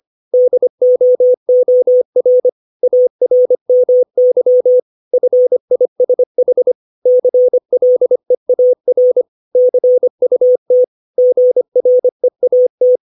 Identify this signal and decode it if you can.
Morse code, heard as a keyed tone, about 25 wpm, 500 Hz. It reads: DOOR ARMY FISH CLEAR CUT GREAT
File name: 632 words / 632 Words 25WPM